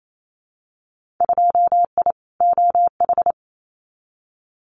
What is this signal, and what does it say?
Morse code, heard as a keyed tone, about 28 wpm, 700 Hz. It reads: 2I OH
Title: Winter Field Day Exchanges 28wpm